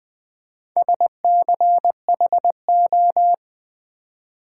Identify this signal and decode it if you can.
Morse code, heard as a keyed tone, about 20 wpm, 700 Hz. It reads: SCHO
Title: Build A Word - 4 Letters At A Time - 11-Letter Words - 20wpm